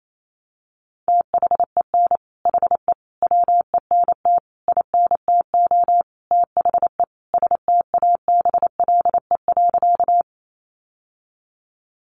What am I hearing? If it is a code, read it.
Morse code, heard as a keyed tone, about 28 wpm, 700 Hz. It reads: THEN HE WENT INTO THE STABLE.
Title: Sentences from Top 1500 Words 28wpm